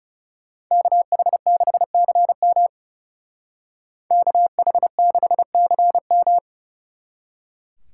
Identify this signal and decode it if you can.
Morse code, heard as a keyed tone, about 37 wpm, 700 Hz. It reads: KH6CM KH6CM
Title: Speed Racing - USA 2x2 Call Signs 30wpm